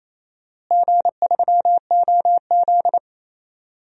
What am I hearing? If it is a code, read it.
Morse code, heard as a keyed tone, about 28 wpm, 700 Hz. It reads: G3OZ
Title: DXCC-100 Call Signs - 4 Characters - Repeated 2x - 28wpm